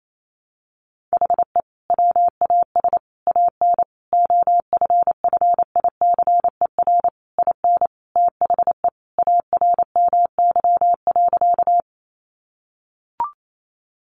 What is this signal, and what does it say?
Morse code, heard as a keyed tone, about 28 wpm, 700 Hz. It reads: HE WAS AN OFFICER IN THE ARMY.
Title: Sentences from Top 1200 Words 28wpm